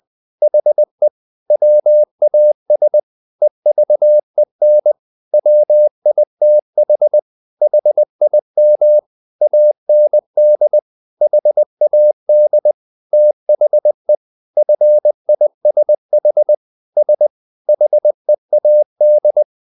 Morse code, heard as a keyed tone, about 20 wpm, 600 Hz. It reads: HE WAS EVEN WITH HIM AND HAD THE FISH S HEAD
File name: the old man and the sea part 2 20wpm